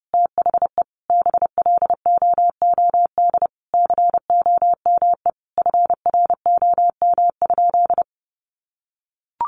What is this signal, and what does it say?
Morse code, heard as a keyed tone, about 30 wpm, 700 Hz. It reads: THE BLOOD COME FROM?
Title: Sentences from Top 1100 Words 30wpm